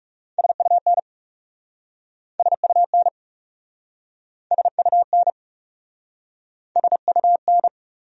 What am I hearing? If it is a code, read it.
Morse code, heard as a keyed tone, about 40 wpm, 700 Hz. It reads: SUN SUN SUN SUN